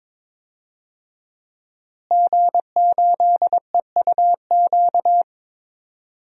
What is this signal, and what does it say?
Morse code, heard as a keyed tone, about 22 wpm, 700 Hz. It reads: G8EUQ
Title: DXCC-100 Call Signs - 5 Characters - Repeated 3x - 22wpm